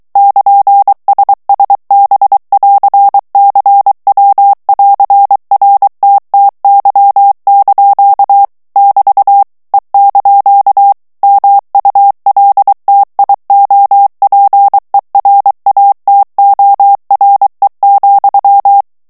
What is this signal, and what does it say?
Morse code, heard as a keyed tone, about 23 wpm, 800 Hz. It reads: <KN>SSB<AR>CW<AR>RTTY) <BT> E) MULTIOPERATORE,
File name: ITALIEN texte clair 2 vitesse 22 mots minute